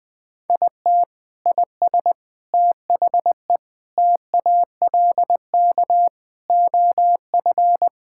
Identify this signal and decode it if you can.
Morse code, heard as a keyed tone, about 20 wpm, 700 Hz. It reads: IT IS THE TALK OF
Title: Sentences with 4-Word Phrases - 20wpm